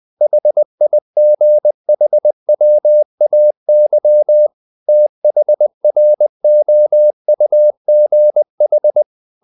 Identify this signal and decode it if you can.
Morse code, heard as a keyed tone, about 20 wpm, 600 Hz. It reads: HIGHWAY THROUGH
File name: the old man and the sea part 2 20wpm